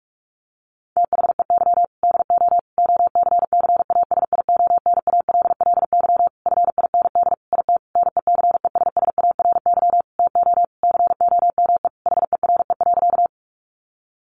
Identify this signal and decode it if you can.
Morse code, heard as a keyed tone, about 44 wpm, 700 Hz. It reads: THEY DO OCCASIONALLY FIND IT NECESSARY TO COME HERE.
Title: Sentences from Top 1300 Words 45wpm